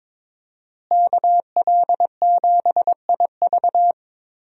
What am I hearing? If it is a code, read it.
Morse code, heard as a keyed tone, about 22 wpm, 700 Hz. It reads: KL7IV
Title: USA Call Signs - 5 Characters - Repeated 3x - 22wpm